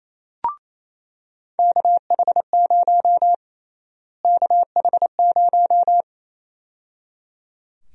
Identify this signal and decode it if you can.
Morse code, heard as a keyed tone, about 28 wpm, 700 Hz. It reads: KH0 KH0
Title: USA Call Sign Prefixes - Repeated 2x - 28wpm